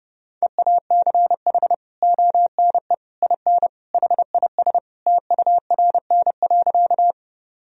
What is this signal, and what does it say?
Morse code, heard as a keyed tone, about 30 wpm, 700 Hz. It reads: EACH ONE IN HIS TURN.
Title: Sentences from Top 1000 Words - Review Part 1 - 30wpm